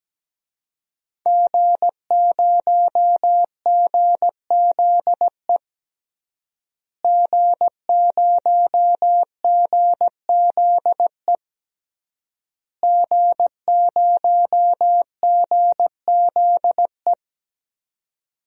Morse code, heard as a keyed tone, about 17 wpm, 700 Hz. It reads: G0GZE G0GZE G0GZE
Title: DXCC-100 Call Signs - 5 Characters - Repeated 3x - 17wpm